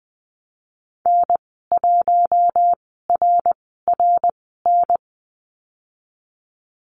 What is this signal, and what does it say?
Morse code, heard as a keyed tone, about 20 wpm, spaced as for 15 wpm, 700 Hz. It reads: N1RRN
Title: 055 Lesson 10g - Letter r - Call Signs 20wpm